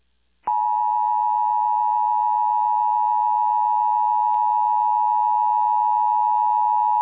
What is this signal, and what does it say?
Emergency Alert System (SAME) tones (the 853 and 960 Hz attention tone). It is heard from 0.8 to 6.4 s.
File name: eas / KUSO / KUSO